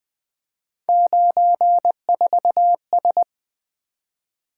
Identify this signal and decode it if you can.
Morse code, heard as a keyed tone, about 24 wpm, 700 Hz. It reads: 94S
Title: Speed-Racing - Three Letter-Number Combinations - 20wpm